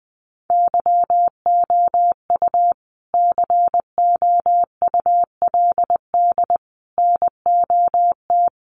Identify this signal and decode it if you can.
Morse code, heard as a keyed tone, about 20 wpm, 700 Hz. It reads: YOU COULD NOT
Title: Sentences from Top 1400 Words 20wpm